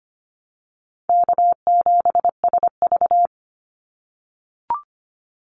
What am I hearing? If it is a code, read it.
Morse code, heard as a keyed tone, about 25 wpm, 700 Hz. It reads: K7SV
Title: Call Signs - 25wpm